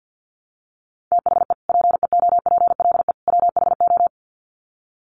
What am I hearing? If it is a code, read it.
Morse code, heard as a keyed tone, about 50 wpm, 700 Hz. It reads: THE PEOPLE WHO
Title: Sets of 3 Words 50wpm